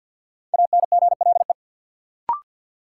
Morse code, heard as a keyed tone, about 46 wpm, 700 Hz. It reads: ANGLE
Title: Sentences from Top 1500 Words - New Word Drill Alt - 50wpm